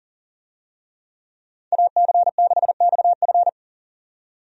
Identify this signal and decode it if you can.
Morse code, heard as a keyed tone, about 40 wpm, 700 Hz. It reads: AC6XF
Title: USA Call Signs - 5 Characters - 40wpm